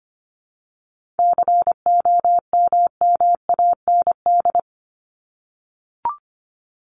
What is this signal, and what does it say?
Morse code, heard as a keyed tone, about 25 wpm, 700 Hz. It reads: COMMAND
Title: Top 1501-1750 Words 25wpm